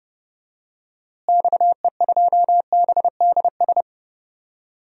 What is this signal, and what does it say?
Morse code, heard as a keyed tone, about 32 wpm, 700 Hz. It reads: XE2BDS
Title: DXCC-100 Call Signs - 6 Characters - Repeated 3x - 30wpm